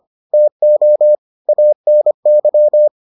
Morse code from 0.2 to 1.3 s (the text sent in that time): TO